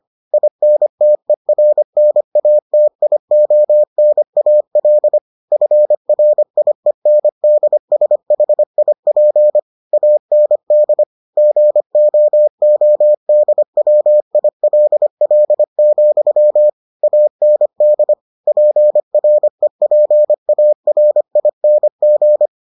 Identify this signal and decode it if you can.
Morse code, heard as a keyed tone, about 25 wpm, 600 Hz. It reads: INTERNATIONAL FRIENDSHIP AND GOODWILL, AND PREPARING